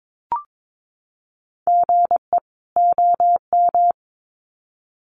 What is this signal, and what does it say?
Morse code, heard as a keyed tone, about 22 wpm, 700 Hz. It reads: GE OM
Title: Common QSO Phrases 22wpm